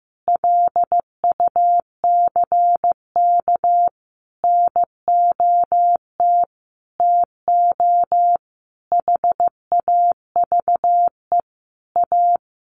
Morse code, heard as a keyed tone, about 14 wpm, 700 Hz. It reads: LUCK NOT TO HAVE A
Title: Sentences from Top 1500 Words 15wpm